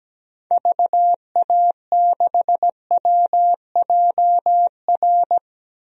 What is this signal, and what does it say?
Morse code, heard as a keyed tone, about 17 wpm, 700 Hz. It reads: VA6WJR
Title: DXCC-100 Call Signs - 6 Characters - Repeated 2x - 17wpm